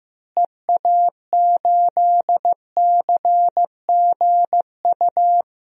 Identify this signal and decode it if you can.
Morse code, heard as a keyed tone, about 15 wpm, 700 Hz. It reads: EA8CGU
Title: DXCC-100 Call Signs - 6 Characters - Repeated 2x - 15wpm